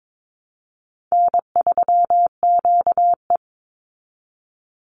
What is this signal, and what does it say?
Morse code, heard as a keyed tone, about 22 wpm, 690 Hz. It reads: N3QE
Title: Easy Call Signs Repeated 2x - 22wpm